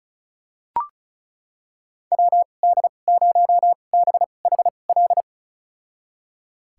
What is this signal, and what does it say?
Morse code, heard as a keyed tone, about 36 wpm, spaced as for 25 wpm, 700 Hz. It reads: WD0BHL
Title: USA Call Signs - 2x Character Spacing - 35wpm